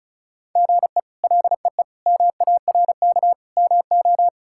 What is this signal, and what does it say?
Morse code, heard as a keyed tone, about 34 wpm, 700 Hz. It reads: GE LEE MARK MO
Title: K1USN SST QSO Practice - 35wpm